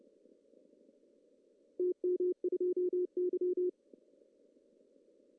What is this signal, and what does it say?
Morse code, heard as a keyed tone, about 30 wpm, 410 Hz. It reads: TM2Y
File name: CW / cqww-cw-k5tr